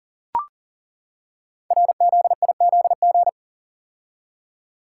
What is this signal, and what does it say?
Morse code, heard as a keyed tone, about 41 wpm, 700 Hz. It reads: RZIZG
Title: Code Groups - 5 Letters - 40wpm